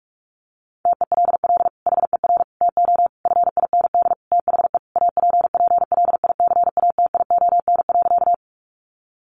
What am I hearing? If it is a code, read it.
Morse code, heard as a keyed tone, about 44 wpm, 700 Hz. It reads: TELL HER TO FIND THE APPLICATION.
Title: Sentences from Top 1200 Words 45wpm